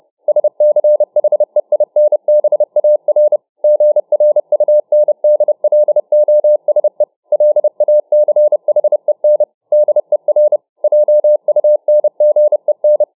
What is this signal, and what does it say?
Morse code, heard as a keyed tone, about 31 wpm, 600 Hz. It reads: SCHEINBAR GRUNDLOSE LACHEN DER JUNGEN